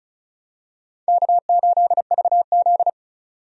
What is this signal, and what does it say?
Morse code, heard as a keyed tone, about 36 wpm, 700 Hz. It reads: K8VZ
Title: USA Call Signs - 4 Characters - 35wpm